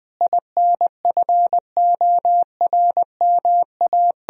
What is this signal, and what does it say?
Morse code, heard as a keyed tone, about 21 wpm, 700 Hz. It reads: INFORMA
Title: Sentences from Top 1000 Words - Review Part 4 - 20wpm